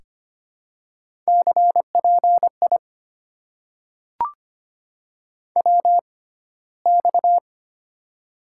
Morse code, heard as a keyed tone, about 25 wpm, spaced as for 15 wpm, 700 Hz. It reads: CPI WX
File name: Top 90 QSO Elements - 6x Character Spacing - 25wpm